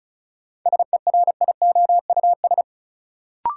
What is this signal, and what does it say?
Morse code, heard as a keyed tone, about 34 wpm, 700 Hz. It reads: SERIOUS
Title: Sentences from Top 1100 Words - New Word Drill - 35wpm